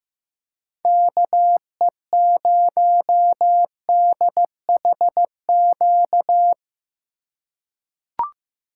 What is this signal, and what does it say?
Morse code, heard as a keyed tone, about 15 wpm, 700 Hz. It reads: KE0DHQ
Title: USA Call Signs - 6 Characters - 3x Character Spacing - 15wpm